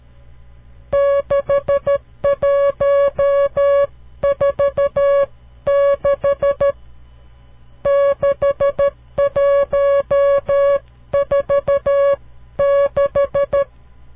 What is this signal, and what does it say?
Morse code, heard as a keyed tone, about 13 wpm, 560 Hz. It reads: 6146 6146
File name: CW Learning Programs / farnsworth / unit 5